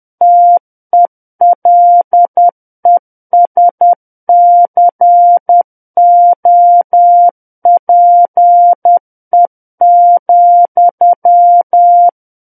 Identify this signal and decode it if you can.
Morse code, heard as a keyed tone, about 10 wpm, 700 Hz. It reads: TELESCOPE,
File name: Book One / 01-Chapter 1 - 10 WPM